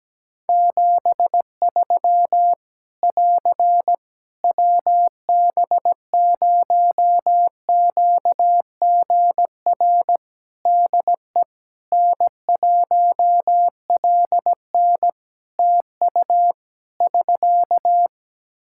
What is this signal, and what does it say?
Morse code, heard as a keyed tone, about 17 wpm, 700 Hz. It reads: 73 <AR> WB0QGR DE N1LN TU <SK>
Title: Practice QSO - Exchange 4 - Ending - 17wpm